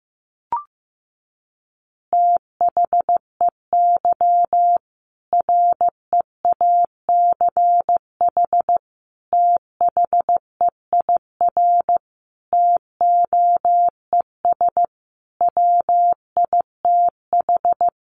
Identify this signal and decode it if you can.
Morse code, heard as a keyed tone, about 15 wpm, 700 Hz. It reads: THEY REACH THEIR TOES WITH
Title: Easy Phrases 15wpm